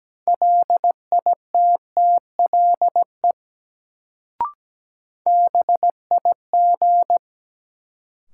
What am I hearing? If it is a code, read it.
Morse code, heard as a keyed tone, about 17 wpm, 700 Hz. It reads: LITTLE BIG
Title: Sentences from Top 200 Words - New Word Drill - 17wpm